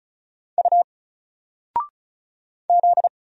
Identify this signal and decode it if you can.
Morse code, heard as a keyed tone, about 36 wpm, 700 Hz. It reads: U Z
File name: Single-Letter-Number-35wpm